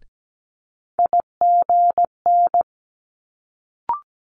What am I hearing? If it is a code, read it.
Morse code, heard as a keyed tone, about 17 wpm, 700 Hz. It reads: IGN
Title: Multi-letter Phonemes Quick Fire 17wpm